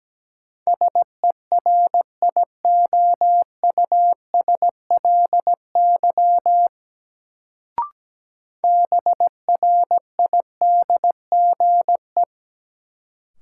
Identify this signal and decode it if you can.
Morse code, heard as a keyed tone, about 17 wpm, 700 Hz. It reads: SERIOUSLY BRIDGE
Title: Sentences from Top 1200 Words - New Word Drill Alt - 17wpm